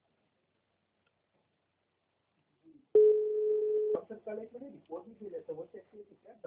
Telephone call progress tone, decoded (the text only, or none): ringback tone